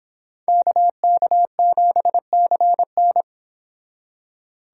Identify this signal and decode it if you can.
Morse code, heard as a keyed tone, about 29 wpm, 700 Hz. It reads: KK7CN